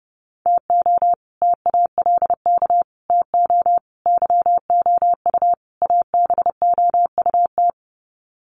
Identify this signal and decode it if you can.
Morse code, heard as a keyed tone, about 30 wpm, 700 Hz. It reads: TO TALK TO YOU ABOUT
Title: Sets of 5 Words Encore 30wpm